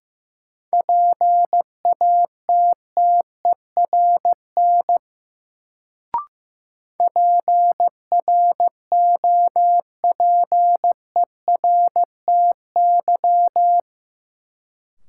Morse code, heard as a keyed tone, about 15 wpm, 700 Hz. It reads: PATTERN PROPERTY
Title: Sentences from Top 1000 Words - New Word Drill - 15wpm